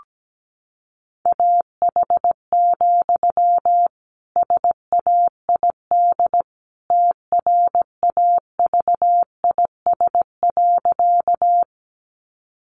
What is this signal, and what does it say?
Morse code, heard as a keyed tone, about 17 wpm, 700 Hz. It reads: AH, SAID TRAVIS.
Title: A Sound Of Thunder 17wpm